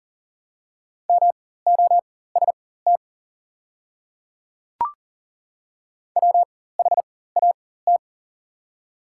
Morse code, heard as a keyed tone, about 42 wpm, spaced as for 19 wpm, 700 Hz. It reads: MOST WHAT
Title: 4x Character Spacing - Top 100 Words - 40wpm